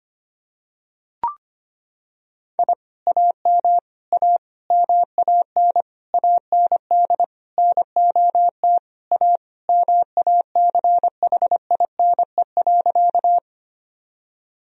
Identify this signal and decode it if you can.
Morse code, heard as a keyed tone, about 25 wpm, 700 Hz. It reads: I AM A MAN AND NOT A MACHINE.
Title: Sentences from Top 1000 Words - Review Part 2 - 25wpm